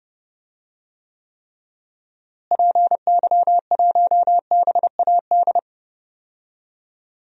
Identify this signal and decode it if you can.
Morse code, heard as a keyed tone, about 30 wpm, 700 Hz. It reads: PY1BAD
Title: DXCC-100 Call Signs - 6 Characters - Repeated 3x - 30wpm